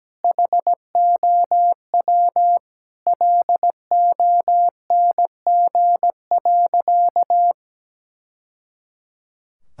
Morse code, heard as a keyed tone, about 17 wpm, 700 Hz. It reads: HOW LONG.
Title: Sentences from Top 1000 Words - Review Part 6 - 17wpm